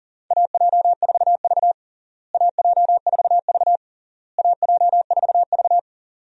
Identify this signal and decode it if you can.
Morse code, heard as a keyed tone, about 40 wpm, 700 Hz. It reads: AJ4V AJ4V AJ4V
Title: USA Call Signs - 4 Characters - Repeated 3x - 40wpm